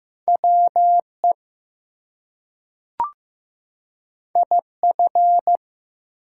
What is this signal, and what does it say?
Morse code, heard as a keyed tone, about 15 wpm, 700 Hz. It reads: WE IF
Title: Two Letter Words - 15wpm